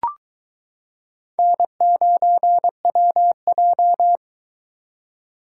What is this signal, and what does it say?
Morse code, heard as a keyed tone, about 23 wpm, 700 Hz. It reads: N9WJ